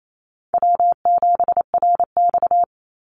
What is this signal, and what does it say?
Morse code, heard as a keyed tone, about 28 wpm, 700 Hz. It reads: W7RX
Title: Call Signs - 28wpm